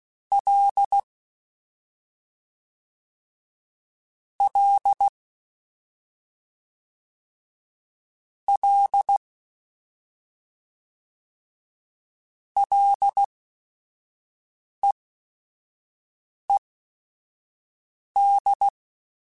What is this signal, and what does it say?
Morse code, heard as a keyed tone, about 16 wpm, spaced as for 5 wpm, 800 Hz. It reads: L L L LEED
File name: Ls / Single L m3s / Single L @5